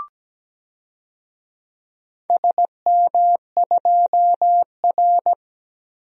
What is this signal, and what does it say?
Morse code, heard as a keyed tone, about 17 wpm, 700 Hz. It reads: SM2R
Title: DXCC-100 Call Signs - 4 Characters - 17wpm